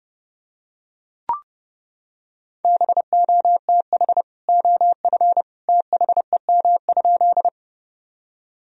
Morse code, heard as a keyed tone, about 32 wpm, 700 Hz. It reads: BOTH OF THEM?
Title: Sentences from Top 1000 Words - Review Part 1 - 30wpm